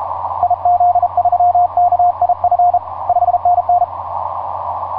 Morse code, heard as a keyed tone, about 32 wpm, 710 Hz, with radio noise. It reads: IZ3KIF 5NN
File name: j38xx mw0cri etc 15m cw 18feb2010 1532